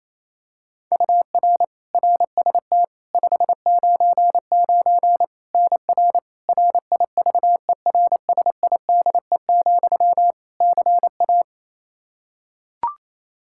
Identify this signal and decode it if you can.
Morse code, heard as a keyed tone, about 27 wpm, 700 Hz. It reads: UR RST 599 NR RIVERSIDE, CA
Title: Practice QSO - Exchange 1 - Beginning - 28wpm